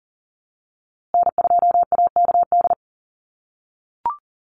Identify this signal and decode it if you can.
Morse code, heard as a keyed tone, about 40 wpm, 700 Hz. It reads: N2AKD
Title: USA Call Signs Repeated 3x 40wpm